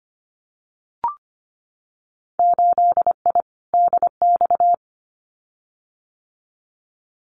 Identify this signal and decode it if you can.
Morse code, heard as a keyed tone, about 25 wpm, 690 Hz. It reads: 8I DX
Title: Winter Field Day Exchanges Encore 25wpm